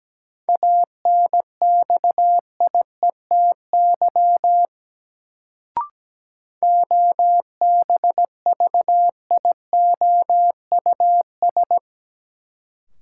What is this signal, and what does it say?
Morse code, heard as a keyed tone, about 17 wpm, 700 Hz. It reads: ANXIETY OBVIOUS
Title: Sentences from Top 1600 Words - New Word Drill - 17wpm